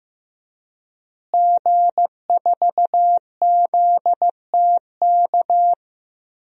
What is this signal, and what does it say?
Morse code, heard as a keyed tone, about 15 wpm, 700 Hz. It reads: G4ZTK
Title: DXCC-100 Call Signs - 5 Characters - Repeated 2x - 15wpm